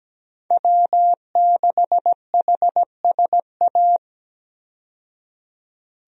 Morse code, heard as a keyed tone, about 17 wpm, 700 Hz. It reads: W6HSA
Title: USA Call Signs - 5 Characters - 17wpm